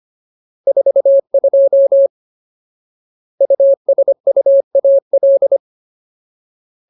Morse code, heard as a keyed tone, about 25 wpm, 550 Hz. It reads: 42 USUAL
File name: Ultimate Word List / 25WPM Ultimate